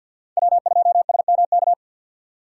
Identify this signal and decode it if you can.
Morse code, heard as a keyed tone, about 47 wpm, 700 Hz. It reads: W2SMK
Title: USA Call Signs - 5 Characters - 6x Character Spacing - 50wpm